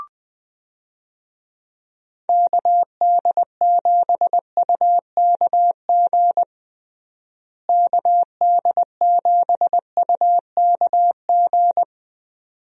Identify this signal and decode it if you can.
Morse code, heard as a keyed tone, about 20 wpm, 700 Hz. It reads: KD7UKG KD7UKG
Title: USA Call Signs - 6 Characters - Repeated 2x - 20wpm